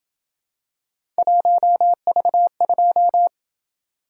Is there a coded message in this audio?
Yes: Morse code from 1.2 to 3.3 s, heard as a keyed tone, about 28 wpm, 700 Hz. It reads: 1V2